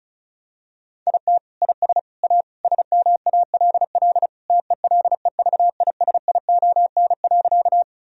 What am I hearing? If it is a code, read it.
Morse code, heard as a keyed tone, about 36 wpm, 700 Hz. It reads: IT IS A SMALL TELEVISION.
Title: Sentences from Top 1000 Words - Review Part 4 - 35wpm